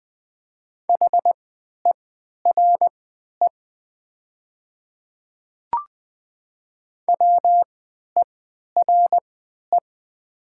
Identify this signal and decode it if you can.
Morse code, heard as a keyed tone, about 20 wpm, spaced as for 11 wpm, 700 Hz. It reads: HERE WERE
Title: 3x Character Spacing - Top 300 Words - 20wpm